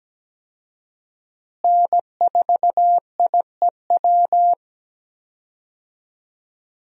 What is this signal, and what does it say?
Morse code, heard as a keyed tone, about 17 wpm, 700 Hz. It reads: N4IEW
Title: USA Call Signs - 5 Characters - Repeated 2x - 17wpm